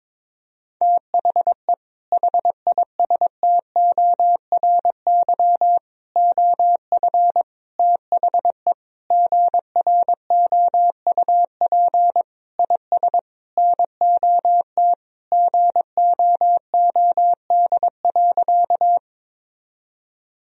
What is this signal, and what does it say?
Morse code, heard as a keyed tone, about 22 wpm, 700 Hz. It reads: THE HISTORY OF THE GROUP IS NOT GOOD.
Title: Sentences from Top 1000 Words - Review Part 1 - 22wpm